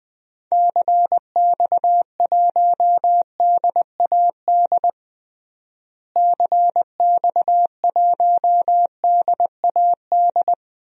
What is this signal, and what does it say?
Morse code, heard as a keyed tone, about 21 wpm, 700 Hz. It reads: CX1DAD CX1DAD
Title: DXCC-100 Call Signs - 6 Characters - Repeated 2x - 20wpm